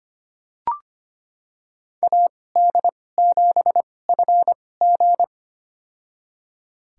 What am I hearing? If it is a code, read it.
Morse code, heard as a keyed tone, about 25 wpm, spaced as for 18 wpm, 700 Hz. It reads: AD7FG
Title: USA Call Signs - 2x Character Spacing - 25wpm